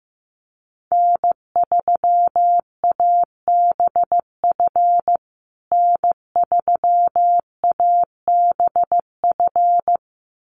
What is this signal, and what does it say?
Morse code, heard as a keyed tone, about 15 wpm, 700 Hz. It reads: N3ABF N3ABF
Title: USA Call Signs Repeated 2x 15wpm